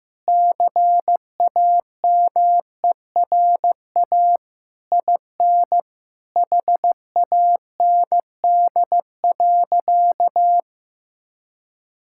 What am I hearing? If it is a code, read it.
Morse code, heard as a keyed tone, about 15 wpm, 700 Hz. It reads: CAMERA IN HAND.
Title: Sentences from Top 1000 Words - Review Part 5 - 15wpm